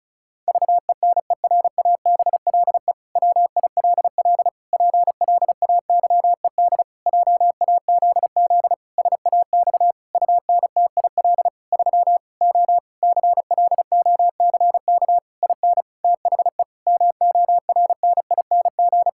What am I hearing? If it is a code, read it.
Morse code, heard as a keyed tone, about 34 wpm, 700 Hz. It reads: VENERABLE WILL PLAYED JAZZ SAX UNTIL 3 O CLOCK IN THE MORNING
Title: Pangram Challenge v2 - 35wpm